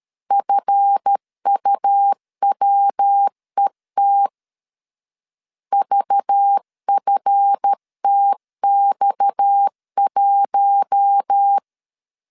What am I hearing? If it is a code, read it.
Morse code, heard as a keyed tone, about 13 wpm, 800 Hz. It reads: FUWET VFTX1
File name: LEZIONI / 8 YJ45 / 8 YJ45 12wpm 2151